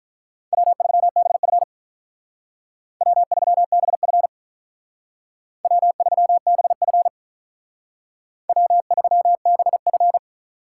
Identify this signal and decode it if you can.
Morse code, heard as a keyed tone, about 46 wpm, 700 Hz. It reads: W3BF W3BF W3BF W3BF